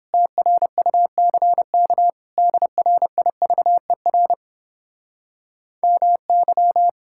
Morse code, heard as a keyed tone, about 31 wpm, 700 Hz. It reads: TRUCK DRIVER MY